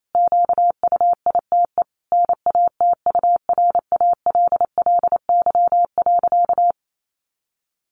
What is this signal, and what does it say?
Morse code, heard as a keyed tone, about 28 wpm, 700 Hz. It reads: QUITE NATURALLY.
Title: Sentences from Top 1300 Words 28wpm